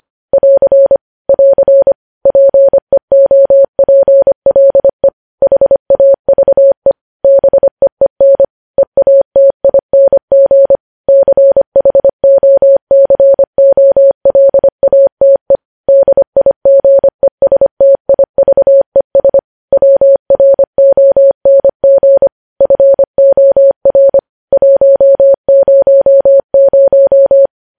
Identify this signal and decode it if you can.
Morse code, heard as a keyed tone, about 25 wpm, 550 Hz. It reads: <AR> <AR> PEOPLE HAVE BEEN EATING CHOCOLATE DIGESTIVES WRONG FOR 100
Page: News Headlines in Morse Code at 25 WPM